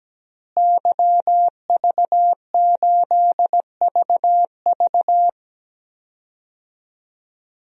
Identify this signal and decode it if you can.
Morse code, heard as a keyed tone, about 17 wpm, 700 Hz. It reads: YV8VV
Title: DXCC-100 Call Signs - 5 Characters - Repeated 2x - 17wpm